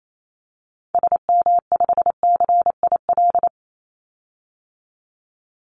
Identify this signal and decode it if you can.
Morse code, heard as a keyed tone, about 28 wpm, 700 Hz. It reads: SM5CIL
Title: Call Signs - 28wpm